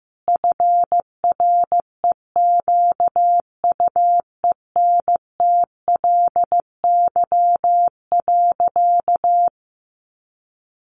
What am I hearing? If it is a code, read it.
Morse code, heard as a keyed tone, about 14 wpm, 700 Hz. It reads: FREQUENTLY.
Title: Sentences from Top 1300 Words 15wpm